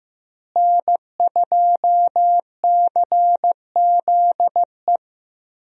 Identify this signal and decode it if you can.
Morse code, heard as a keyed tone, about 15 wpm, 700 Hz. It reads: N2CZE
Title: USA Call Signs - 5 Characters - Repeated 2x - 15wpm